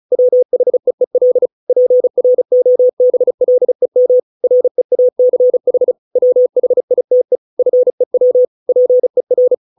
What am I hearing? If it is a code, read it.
Morse code, heard as a keyed tone, about 34 wpm, 500 Hz. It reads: WHEEL PROBLEM REACH WHITE FEW PER